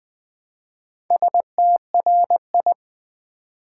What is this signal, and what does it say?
Morse code, heard as a keyed tone, about 20 wpm, 700 Hz. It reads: STRI
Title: Build A Word - 1 Letter At A Time - 5-Letter Words - 20wpm